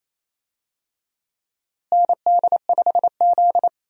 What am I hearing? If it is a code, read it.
Morse code, heard as a keyed tone, about 28 wpm, 700 Hz. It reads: ND5Z
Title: USA Call Signs - 4 Characters - Repeated 2x - 28wpm